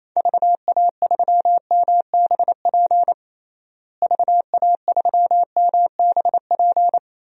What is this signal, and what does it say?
Morse code, heard as a keyed tone, about 28 wpm, 700 Hz. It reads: VA3MBP VA3MBP
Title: DXCC-100 Call Signs - 6 Characters - Repeated 2x - 28wpm